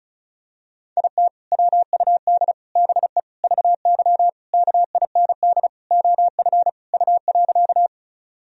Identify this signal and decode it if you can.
Morse code, heard as a keyed tone, about 36 wpm, 700 Hz. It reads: IT WUD BE VY KIND OF U.
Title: Sentences from Top 200 Words and Abbreviations 35wpm